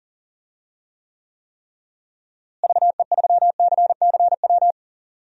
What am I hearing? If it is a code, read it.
Morse code, heard as a keyed tone, about 40 wpm, 700 Hz. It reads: VE3CCW
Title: DXCC-100 Call Signs - 6 Characters - Repeated 3x - 40wpm